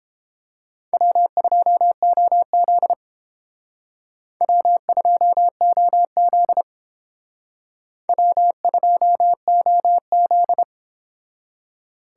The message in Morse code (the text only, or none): W2OZ W2OZ W2OZ